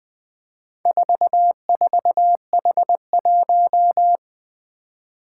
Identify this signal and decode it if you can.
Morse code, heard as a keyed tone, about 21 wpm, 700 Hz. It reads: 44H1
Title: Four Challenging Letter-Number Combinations 20wpm